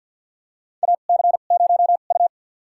Morse code, heard as a keyed tone, about 48 wpm, spaced as for 35 wpm, 700 Hz. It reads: AC0U
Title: USA Call Signs - 4 Characters - 2x Character Spacing - 50wpm